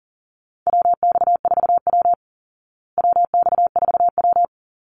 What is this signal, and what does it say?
Morse code, heard as a keyed tone, about 40 wpm, 700 Hz. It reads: WX4W WX4W
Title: Call Signs Repeated 3x - 40wpm